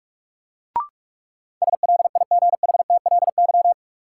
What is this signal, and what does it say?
Morse code, heard as a keyed tone, about 41 wpm, 700 Hz. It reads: SLIGHTLY
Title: Sentences from Top 1300 Words - New Word Drill Alt - 45wpm